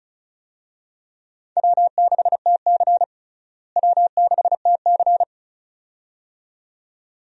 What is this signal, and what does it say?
Morse code, heard as a keyed tone, about 36 wpm, 700 Hz. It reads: W6TC W6TC
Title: USA Call Signs - 4 Characters - Repeated 2x - 35wpm